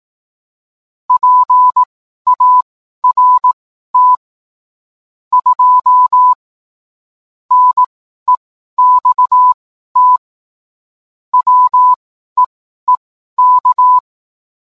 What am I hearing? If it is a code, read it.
Morse code, heard as a keyed tone, about 18 wpm, spaced as for 12 wpm, 1000 Hz. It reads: PART 2 NEXT WEEK